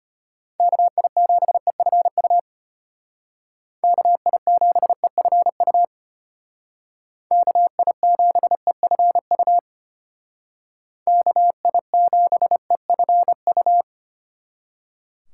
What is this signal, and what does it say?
Morse code, heard as a keyed tone, about 33 wpm, 700 Hz. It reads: KI7EFU KI7EFU KI7EFU KI7EFU